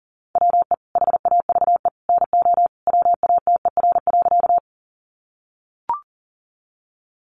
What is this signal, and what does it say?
Morse code, heard as a keyed tone, about 40 wpm, 700 Hz. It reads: WE HAVE NO WATER.
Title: Sentences from Top 100 Words 40wpm